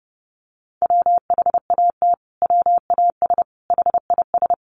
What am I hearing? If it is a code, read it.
Morse code, heard as a keyed tone, about 30 wpm, 700 Hz. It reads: WHAT WAS HIS
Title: Sentences from Top 1400 Words 30wpm